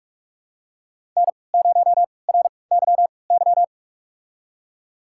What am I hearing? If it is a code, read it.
Morse code, heard as a keyed tone, about 42 wpm, spaced as for 25 wpm, 700 Hz. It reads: N0RYY
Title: USA Call Signs - 3x Character Spacing - 45wpm